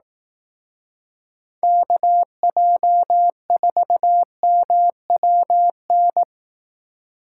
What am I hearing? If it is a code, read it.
Morse code, heard as a keyed tone, about 18 wpm, 700 Hz. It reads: KJ4MWN
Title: Speed Racing - USA Call Signs 15wpm